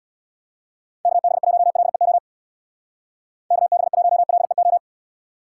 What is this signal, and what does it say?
Morse code, heard as a keyed tone, about 65 wpm, 700 Hz. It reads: KB1LEO KB1LEO